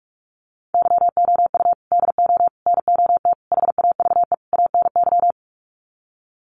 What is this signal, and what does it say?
Morse code, heard as a keyed tone, about 44 wpm, 700 Hz. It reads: YOU DO NOT HAVE ANY
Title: Sets of 5 Words Encore 45wpm